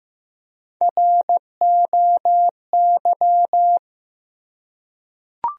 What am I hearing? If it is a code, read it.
Morse code, heard as a keyed tone, about 15 wpm, 700 Hz. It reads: ROY
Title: USA Common Names - Max 3 Letters - 15wpm